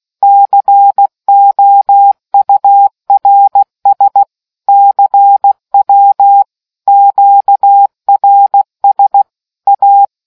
Morse code, heard as a keyed tone, about 16 wpm, 800 Hz. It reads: COURS CW QRS A